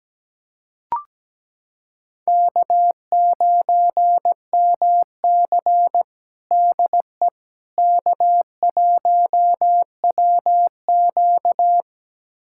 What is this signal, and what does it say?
Morse code, heard as a keyed tone, about 17 wpm, 700 Hz. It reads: K9MC DE K1WQ
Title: USA Callsign DE Callsign - 4 Characters - 17wpm